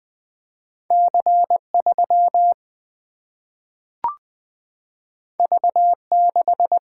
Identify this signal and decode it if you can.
Morse code, heard as a keyed tone, about 20 wpm, 700 Hz. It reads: C 3 V 6
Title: Single-Letter Single-Number 20wpm